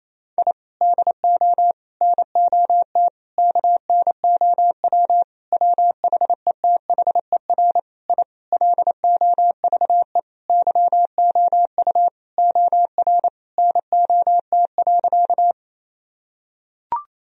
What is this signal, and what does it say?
Morse code, heard as a keyed tone, about 28 wpm, 700 Hz. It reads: I DO NOT KNOW WHETHER I LOVE YOU OR NOT.
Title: Sentences from Top 1000 Words - Review Part 3 - 28wpm